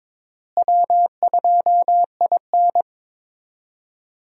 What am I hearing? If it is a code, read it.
Morse code, heard as a keyed tone, about 22 wpm, 700 Hz. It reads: W2IN
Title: USA Call Signs - 4 Characters - Repeated 3x - 22wpm